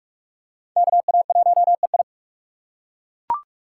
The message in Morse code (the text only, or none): KA1EI